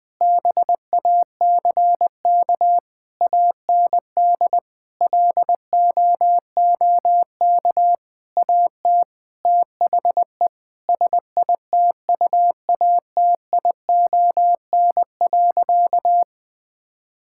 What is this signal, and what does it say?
Morse code, heard as a keyed tone, about 21 wpm, 700 Hz. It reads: BACK AND LOOK AT THE SITUATION.
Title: Sentences with 8-Word Phrases - 20wpm